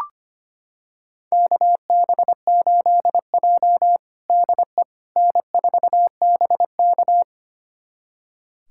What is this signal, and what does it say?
Morse code, heard as a keyed tone, about 25 wpm, 700 Hz. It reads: KB8J DE N4BK
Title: USA Callsign DE Callsign - 4 Characters - 25wpm